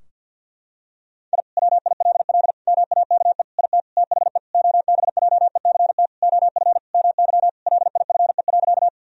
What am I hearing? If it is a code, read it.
Morse code, heard as a keyed tone, about 50 wpm, 700 Hz. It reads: I WILL MAKE IT THE OBJECT OF MY LIFE.
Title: Sentences from Top 1000 Words - Review Part 3 - 50wpm